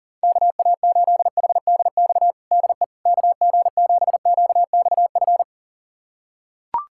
Morse code, heard as a keyed tone, about 37 wpm, 700 Hz. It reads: KA8HDX DE KG7QXF
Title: USA Callsign DE Callsign - 6 Characters - 40wpm